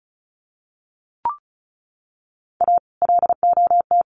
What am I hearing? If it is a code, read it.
Morse code, heard as a keyed tone, about 34 wpm, 700 Hz. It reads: A LOT
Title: Sets of 2 Words 35wpm